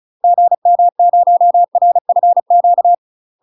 Morse code, heard as a keyed tone, about 36 wpm, 700 Hz. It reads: GM0RFQ
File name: UK MASTER QSOs TRAINING / 35WPM / UK MASTER QSOs 0036